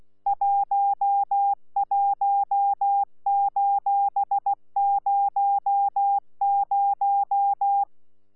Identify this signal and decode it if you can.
Morse code, heard as a keyed tone, about 16 wpm, 800 Hz. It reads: 11:00
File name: Z PI4VRZ / PI4VRZ 14-jun-2025